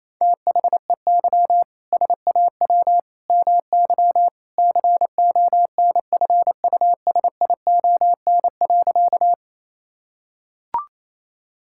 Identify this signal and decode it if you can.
Morse code, heard as a keyed tone, about 28 wpm, 700 Hz. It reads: THEY SAW MY CONFUSION.
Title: Sentences from Top 1600 Words 28wpm